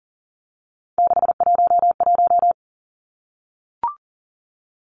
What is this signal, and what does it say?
Morse code, heard as a keyed tone, about 42 wpm, 690 Hz. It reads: 611
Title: Three Challenging Letter-Number Combinations 40wpm